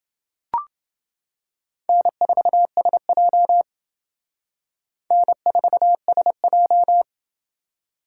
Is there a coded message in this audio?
Yes: Morse code from 1.9 to 7.0 s, heard as a keyed tone, about 28 wpm, 700 Hz. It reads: N4SJ N4SJ